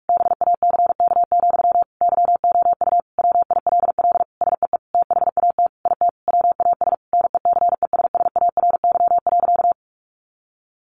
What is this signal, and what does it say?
Morse code, heard as a keyed tone, about 44 wpm, 700 Hz. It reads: BACK, YOU WILL SEE THAT IT WAS NECESSARY.
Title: Sentences from Top 800 Words 45wpm